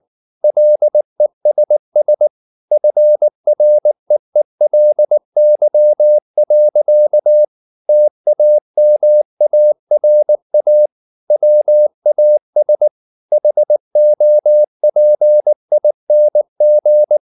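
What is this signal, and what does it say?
Morse code, heard as a keyed tone, about 19 wpm, 600 Hz. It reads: LESS FREELY. TAMARA WAS HOPING
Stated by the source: Morse code at 19 wpm from CW med Gnister, reading Ken Follett's novel Never in English.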